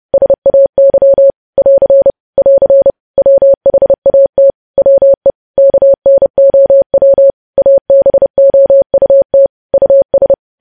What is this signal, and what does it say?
Morse code, heard as a keyed tone, about 30 wpm, 550 Hz. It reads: SAY <AR> <AR> WHAT WE KNOW ABOUT US
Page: news headlines in morse code at 30 wpm